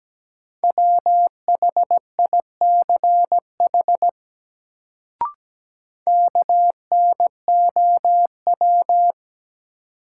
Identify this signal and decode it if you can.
Morse code, heard as a keyed tone, about 17 wpm, 700 Hz. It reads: WHICH KNOW
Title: Sentences from Top 100 Words - New Word Drill - 17wpm